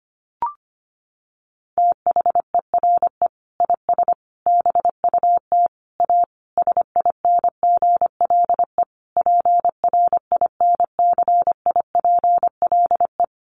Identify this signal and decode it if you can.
Morse code, heard as a keyed tone, about 25 wpm, 700 Hz. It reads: THERE IS BUT A SINGLE PRINCIPLE
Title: Sentences from Top 1400 Words 25wpm